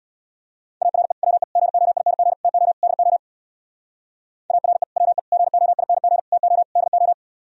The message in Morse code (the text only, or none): ARE WE GOING TO DO ARE WE GOING TO DO